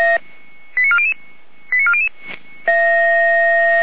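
Five-tone selective call (ZVEI), heard as quick runs of five tones, tones of about 80 ms. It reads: 89400 89400
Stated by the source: ZVEI five-tone selective call received with an SDR.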